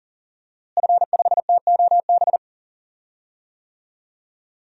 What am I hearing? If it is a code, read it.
Morse code, heard as a keyed tone, about 40 wpm, 700 Hz. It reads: F5TOB
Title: DXCC-100 Call Signs - 5 Characters - Repeated 3x - 40wpm